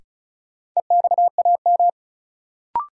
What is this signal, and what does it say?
Morse code, heard as a keyed tone, about 34 wpm, 700 Hz. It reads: EXAM
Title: Sentences from Top 1000 Words - Review Part 5 - New Word Drill - 35wpm